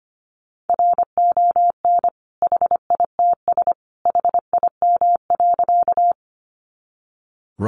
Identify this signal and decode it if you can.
Morse code, heard as a keyed tone, about 25 wpm, 700 Hz. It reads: RON HITS HIM.
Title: Additional Easy Phrases 25wpm